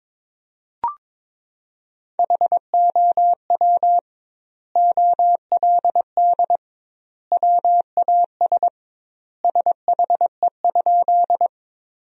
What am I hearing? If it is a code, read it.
Morse code, heard as a keyed tone, about 22 wpm, 700 Hz. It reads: HOW OLD WAS SHE?
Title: 2x Word-Spacing - Sentences from Top 200 Words - 22wpm